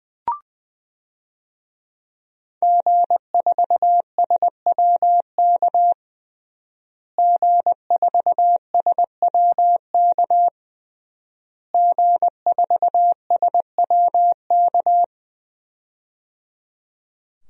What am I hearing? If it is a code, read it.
Morse code, heard as a keyed tone, about 20 wpm, 700 Hz. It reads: G4SWK G4SWK G4SWK
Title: DXCC-100 Call Signs - 5 Characters - Repeated 3x - 20wpm